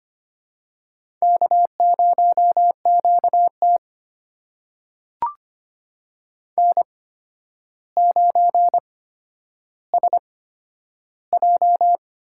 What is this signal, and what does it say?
Morse code, heard as a keyed tone, about 25 wpm, spaced as for 11 wpm, 700 Hz. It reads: K0QT N9SJ
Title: USA Call Signs - 4 Characters - 8x Character Spacing - 25wpm